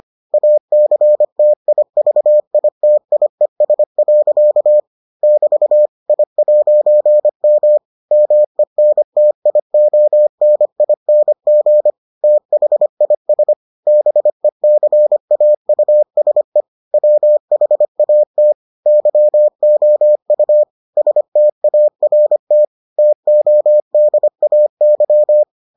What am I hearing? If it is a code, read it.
Morse code, heard as a keyed tone, about 25 wpm, 600 Hz. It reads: ACTIVITIES. <BT> I'M MENTIONING THIS BECAUSE WHAT YOU START TODAY